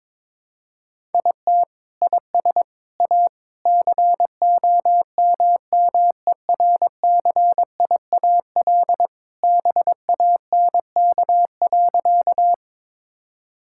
Morse code, heard as a keyed tone, about 21 wpm, 700 Hz. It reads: IT IS A COMMERCIAL BANK.
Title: Sentences from Top 1000 Words - Review Part 6 - 22wpm